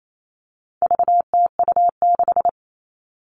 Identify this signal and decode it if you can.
Morse code, heard as a keyed tone, about 28 wpm, 700 Hz. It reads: VTU6
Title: All Cut Numbers - 4-Digit 28wpm